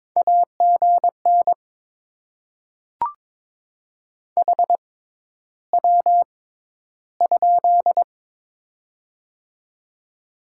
Morse code, heard as a keyed tone, about 22 wpm, spaced as for 11 wpm, 700 Hz. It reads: AGN HW?
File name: Top 90 QSO Elements - 6x Character Spacing - 22wpm